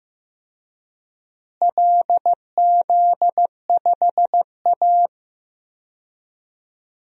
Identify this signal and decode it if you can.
Morse code, heard as a keyed tone, about 15 wpm, 700 Hz. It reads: LZ5A
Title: DXCC-100 Call Signs - 4 Characters - Repeated 3x - 15wpm